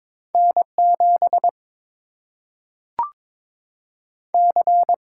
Morse code, heard as a keyed tone, about 22 wpm, 700 Hz. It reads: N 7 C
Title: Single-Letter Single-Number 22wpm